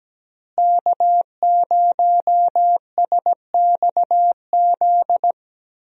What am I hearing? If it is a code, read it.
Morse code, heard as a keyed tone, about 17 wpm, 700 Hz. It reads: K0SXZ
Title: USA Call Signs - 5 Characters - Repeated 3x - 17wpm